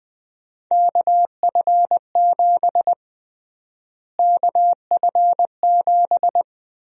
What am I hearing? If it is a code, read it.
Morse code, heard as a keyed tone, about 21 wpm, 700 Hz. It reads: KF7 KF7
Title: USA Call Sign Prefixes - Repeated 2x - 20wpm